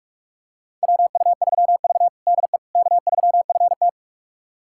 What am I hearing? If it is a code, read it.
Morse code, heard as a keyed tone, about 44 wpm, 700 Hz. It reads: WU3V DE K3FT
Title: USA Callsign DE Callsign - 4 Characters - 45wpm